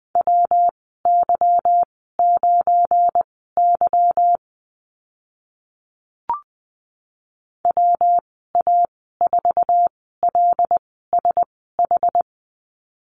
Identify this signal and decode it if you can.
Morse code, heard as a keyed tone, about 20 wpm, spaced as for 14 wpm, 700 Hz. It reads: WY9Y WA4LSH
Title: 179 Lesson 29g - Slash - Call Signs 20wpm